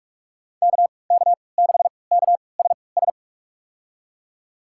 Morse code, heard as a keyed tone, about 46 wpm, spaced as for 25 wpm, 700 Hz. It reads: KK6KSS
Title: USA Call Signs - 6 Characters - 3x Character Spacing - 45wpm